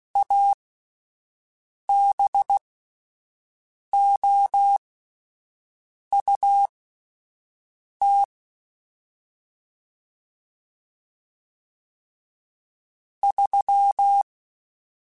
Morse code, heard as a keyed tone, about 16 wpm, spaced as for 5 wpm, 800 Hz. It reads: ABOUT 3
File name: QSOs / QSOs-mp3s / WD0BGP 1@13